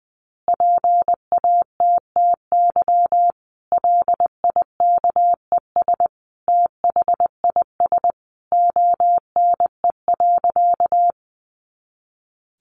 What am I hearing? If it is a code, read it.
Morse code, heard as a keyed tone, about 19 wpm, 700 Hz. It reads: PATTY LIKES THIS ONE.
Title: Additional Easy Phrases 20wpm